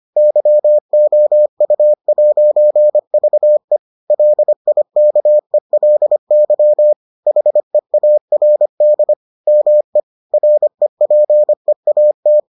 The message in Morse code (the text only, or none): YOU'VE LIKELY HEARD ME REPEAT